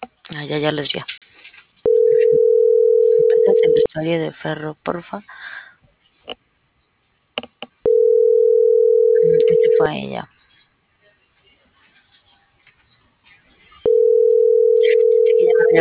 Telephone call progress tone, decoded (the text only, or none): ringback tone